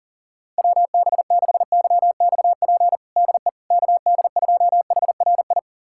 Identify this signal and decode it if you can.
Morse code, heard as a keyed tone, about 40 wpm, 700 Hz. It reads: WB6YXP DE KD2HRI
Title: USA Callsign DE Callsign 40wpm